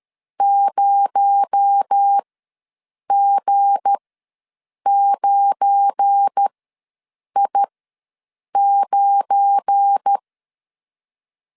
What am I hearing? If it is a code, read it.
Morse code, heard as a keyed tone, about 12 wpm, spaced as for 7 wpm, 800 Hz. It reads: 0G9I9
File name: LEZIONI / 9 2867 ws 12wpm 3431